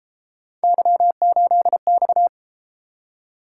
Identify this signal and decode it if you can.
Morse code, heard as a keyed tone, about 33 wpm, 700 Hz. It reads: Y8X